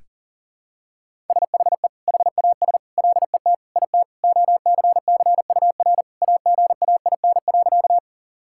Morse code, heard as a keyed tone, about 40 wpm, 700 Hz. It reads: SHE HAS LET IT OCCUR AGAIN.
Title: Sentences from Top 1000 Words - Review Part 6 - 40wpm